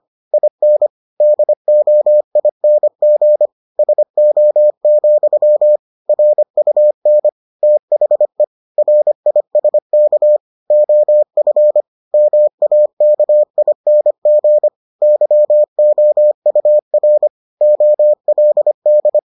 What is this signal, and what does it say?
Morse code, heard as a keyed tone, about 25 wpm, 600 Hz. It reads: IN DOING SO, RUN THE RISK OF MAKING YOUR OLD